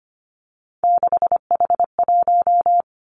Morse code, heard as a keyed tone, about 25 wpm, 700 Hz. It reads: 6H1
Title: Three Challenging Letter-Number Combinations 25wpm